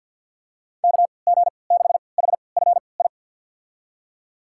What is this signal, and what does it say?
Morse code, heard as a keyed tone, about 51 wpm, spaced as for 28 wpm, 700 Hz. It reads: KG6HFI
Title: USA Call Signs - 3x Character Spacing - 50wpm